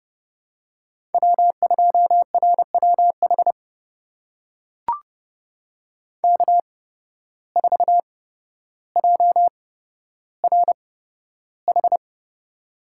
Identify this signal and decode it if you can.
Morse code, heard as a keyed tone, about 30 wpm, spaced as for 13 wpm, 700 Hz. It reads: W2RWH K4JRH
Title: USA Call Signs - 5 Characters - 8x Character Spacing - 30wpm